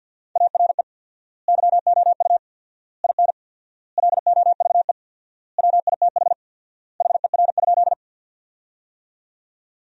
Morse code, heard as a keyed tone, about 47 wpm, 700 Hz. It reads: ARE YOU IN LOVE WITH HER?
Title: 4x Word-Spacing - Sentences from Top 400 Words - 50wpm